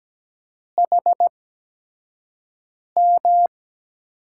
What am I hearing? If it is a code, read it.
Morse code, heard as a keyed tone, about 17 wpm, spaced as for 5 wpm, 700 Hz. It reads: HM
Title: USA Call Signs - 6 Characters - 8x Character Spacing - 17wpm